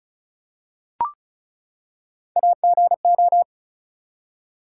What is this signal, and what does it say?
Morse code, heard as a keyed tone, about 36 wpm, 700 Hz. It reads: AGO
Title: Three Letter Words - 35wpm